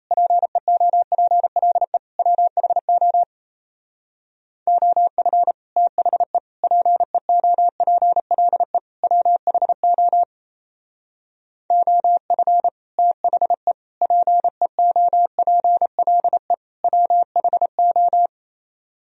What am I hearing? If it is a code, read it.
Morse code, heard as a keyed tone, about 35 wpm, 700 Hz. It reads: PEOPLE WHO OF THE PEOPLE WHO OF THE PEOPLE WHO
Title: Speed-Racing - Sets of 4 Words 28wpm